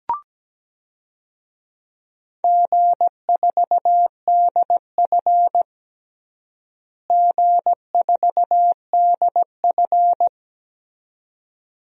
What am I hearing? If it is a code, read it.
Morse code, heard as a keyed tone, about 17 wpm, 700 Hz. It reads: G4DF G4DF
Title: DXCC-100 Call Signs - 4 Characters - Repeated 2x - 17wpm